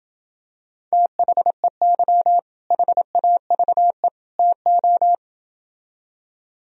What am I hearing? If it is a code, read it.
Morse code, heard as a keyed tone, about 28 wpm, 700 Hz. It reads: THEY HAVE TO